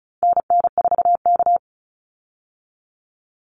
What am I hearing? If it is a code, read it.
Morse code, heard as a keyed tone, about 34 wpm, 700 Hz. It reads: NN4K
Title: Call Signs - 35wpm